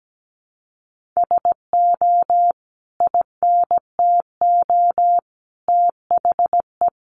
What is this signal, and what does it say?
Morse code, heard as a keyed tone, about 17 wpm, 700 Hz. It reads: SO INTO THE
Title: Sentences from Top 1400 Words 17wpm